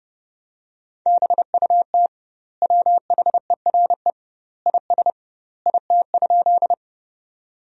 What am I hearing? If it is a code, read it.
Morse code, heard as a keyed tone, about 30 wpm, 700 Hz. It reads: BUT WHERE IS IT?
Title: 2x Word-Spacing - Sentences from Top 200 Words - 30wpm